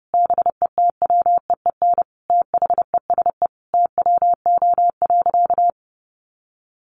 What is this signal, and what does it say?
Morse code, heard as a keyed tone, about 30 wpm, 700 Hz. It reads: BETWEEN THESE TWO.
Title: Sentences from Top 1200 Words 30wpm